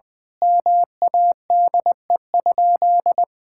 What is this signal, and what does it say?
Morse code, heard as a keyed tone, about 21 wpm, 700 Hz. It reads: MADE?
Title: Sentences from Top 1000 Words - Review Part 6 - 20wpm